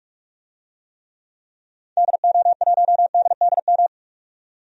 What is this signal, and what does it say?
Morse code, heard as a keyed tone, about 41 wpm, 700 Hz. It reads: DO1DDM
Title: DXCC-100 Call Signs - 6 Characters - 45wpm